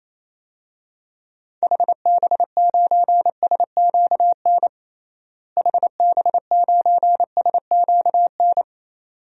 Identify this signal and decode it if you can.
Morse code, heard as a keyed tone, about 28 wpm, 700 Hz. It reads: HB9SQN HB9SQN
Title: DXCC-100 Call Signs - 6 Characters - Repeated 2x - 28wpm